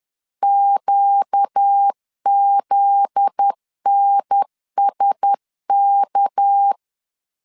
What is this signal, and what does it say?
Morse code, heard as a keyed tone, about 10 wpm, 800 Hz. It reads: QZNSK
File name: LEZIONI / 7 GF9Z / 7 GF9Z 10wpm 2611